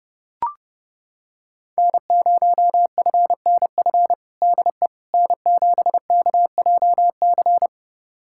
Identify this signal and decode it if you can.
Morse code, heard as a keyed tone, about 30 wpm, 700 Hz. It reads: N0FNF DE N7KJC
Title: USA Callsign DE Callsign - 5 Characters - 30wpm